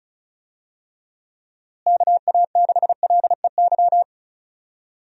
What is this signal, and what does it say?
Morse code, heard as a keyed tone, about 36 wpm, 700 Hz. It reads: KA6LEY
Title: USA Call Signs - 6 Characters - Repeated 2x - 35wpm